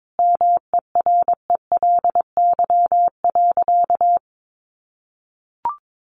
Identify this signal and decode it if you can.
Morse code, heard as a keyed tone, about 22 wpm, 700 Hz. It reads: MERELY.
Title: Sentences from Top 1400 Words 22wpm